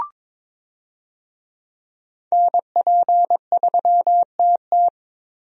Morse code, heard as a keyed tone, about 22 wpm, 700 Hz. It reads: NP3TT
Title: USA Call Signs - 5 Characters - 22wpm